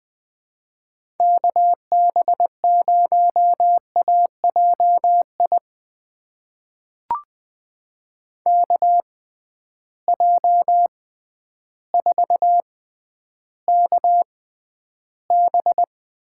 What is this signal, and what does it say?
Morse code, heard as a keyed tone, about 20 wpm, spaced as for 11 wpm, 700 Hz. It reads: KB0AJI KJ4KB
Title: USA Call Signs - 6 Characters - 6x Character Spacing - 20wpm